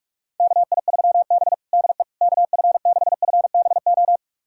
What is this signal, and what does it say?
Morse code, heard as a keyed tone, about 46 wpm, 690 Hz. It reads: KI3B DE KF6FBO
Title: USA Callsign DE Callsign 45wpm